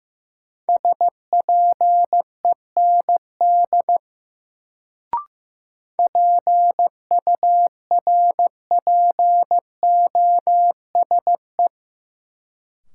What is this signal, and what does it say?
Morse code, heard as a keyed tone, about 15 wpm, 700 Hz. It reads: SPEND PURPOSE
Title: Sentences from Top 800 Words - New Word Drill - 15wpm